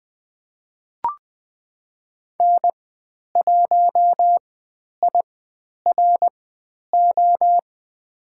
Morse code, heard as a keyed tone, about 20 wpm, spaced as for 10 wpm, 700 Hz. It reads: N1IRO
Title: 109 Lesson 18g - Letter m - Call Signs 20wpm